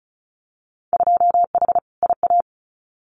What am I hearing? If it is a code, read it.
Morse code, heard as a keyed tone, about 36 wpm, 700 Hz. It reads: 2H IA
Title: Winter Field Day Exchanges Encore 35wpm